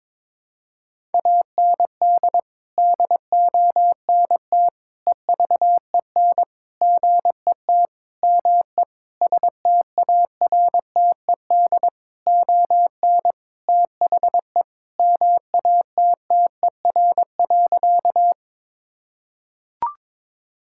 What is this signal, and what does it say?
Morse code, heard as a keyed tone, about 22 wpm, 700 Hz. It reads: AND DONT EVEN GET ME STARTED ON THE MATTER.
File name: practice / 22wpm / Sentences with 7-Word Phrases - 22wpm